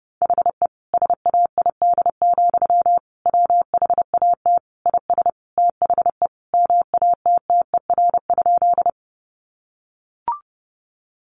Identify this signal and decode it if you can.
Morse code, heard as a keyed tone, about 28 wpm, 700 Hz. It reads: HE SAID, WHAT IS THE MATTER?
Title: Sentences from Top 700 Words 30wpm